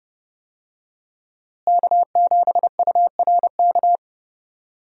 Morse code, heard as a keyed tone, about 32 wpm, 700 Hz. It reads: K7URK
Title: USA Call Signs - 5 Characters - Repeated 2x - 30wpm